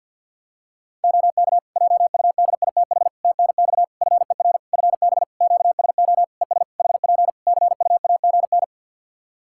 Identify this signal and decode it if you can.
Morse code, heard as a keyed tone, about 49 wpm, 700 Hz. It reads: OK JUDITH TNX FER FB QSO ES HP CUAGN
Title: Practice QSO - Exchange 4 - Ending - 50wpm